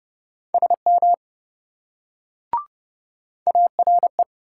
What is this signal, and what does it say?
Morse code, heard as a keyed tone, about 28 wpm, 700 Hz. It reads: SM ARE